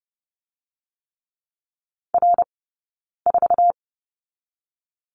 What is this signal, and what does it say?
Morse code, heard as a keyed tone, about 30 wpm, 700 Hz. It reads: R 4
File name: Lesson 20d - Number 3 - Two Characters 30wpm